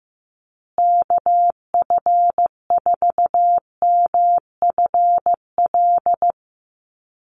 Morse code, heard as a keyed tone, about 15 wpm, 700 Hz. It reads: KF4MFL
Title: USA Call Signs Repeated 2x 15wpm